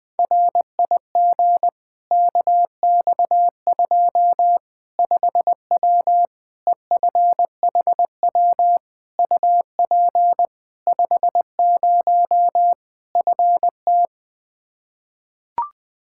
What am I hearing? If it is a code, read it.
Morse code, heard as a keyed tone, about 20 wpm, 700 Hz. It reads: RIG KX2 5W EFHW UP 50 FT
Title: Practice QSO - Exchange 2 - Info - 20wpm